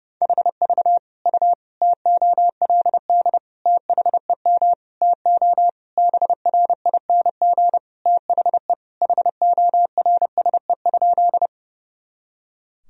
Morse code, heard as a keyed tone, about 30 wpm, 700 Hz. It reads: HV U TOLD THEM TO BRING THE HORSE?
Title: Sentences from Top 500 Words and Abbreviations 30wpm